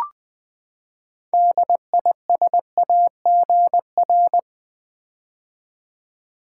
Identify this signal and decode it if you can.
Morse code, heard as a keyed tone, about 20 wpm, 700 Hz. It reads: DISAGR
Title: Build A Word - 6 Letters At A Time - 12-Letter Words - 20wpm